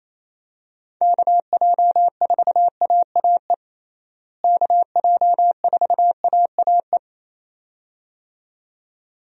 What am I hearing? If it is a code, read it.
Morse code, heard as a keyed tone, about 28 wpm, 700 Hz. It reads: KJ4AAE KJ4AAE
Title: USA Call Signs - 6 Characters - Repeated 2x - 28wpm